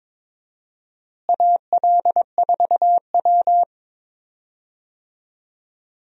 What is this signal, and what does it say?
Morse code, heard as a keyed tone, about 22 wpm, 700 Hz. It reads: AL4W
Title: USA Call Signs - 4 Characters - Repeated 2x - 22wpm